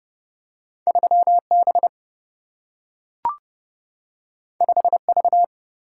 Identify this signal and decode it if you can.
Morse code, heard as a keyed tone, about 31 wpm, 700 Hz. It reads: 3B 5V
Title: Single-Letter Single-Number 30wpm